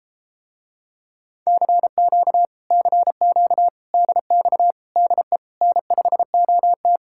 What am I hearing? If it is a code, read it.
Morse code, heard as a keyed tone, about 33 wpm, 700 Hz. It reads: CQ CQ DX DE N5OT